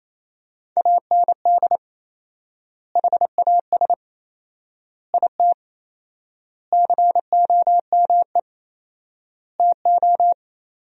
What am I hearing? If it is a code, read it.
Morse code, heard as a keyed tone, about 28 wpm, 700 Hz. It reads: AND HAS IT COME TO
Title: 4x Word-Spacing - Sentences from Top 100 Words - 28wpm